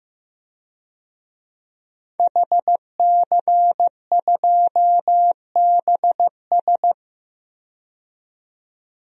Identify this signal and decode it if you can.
Morse code, heard as a keyed tone, about 15 wpm, 700 Hz. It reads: HC2BS
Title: DXCC-100 Call Signs - 5 Characters - 15wpm